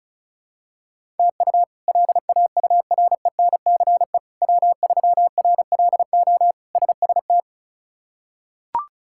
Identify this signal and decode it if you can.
Morse code, heard as a keyed tone, about 34 wpm, 700 Hz. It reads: TU LAURENCE W3RLO SST
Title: K1USN SST QSO Practice - 35wpm